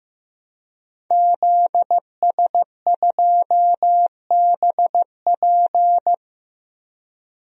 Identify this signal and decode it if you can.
Morse code, heard as a keyed tone, about 15 wpm, 700 Hz. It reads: ZS2BP
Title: DXCC-100 Call Signs - 5 Characters - Repeated 2x - 15wpm